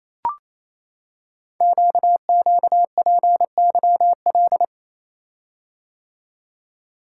Morse code, heard as a keyed tone, about 27 wpm, 700 Hz. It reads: QQPYL
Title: Code Groups - 5 Letters - 28wpm